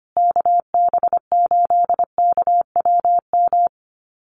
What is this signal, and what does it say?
Morse code, heard as a keyed tone, about 25 wpm, 700 Hz. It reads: KB8KWM
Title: USA Call Signs Repeated 3x 25wpm